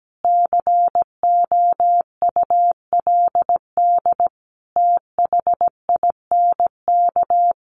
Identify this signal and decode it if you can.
Morse code, heard as a keyed tone, about 17 wpm, 700 Hz. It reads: COULD THINK
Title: Sentences from Top 500 Words 17wpm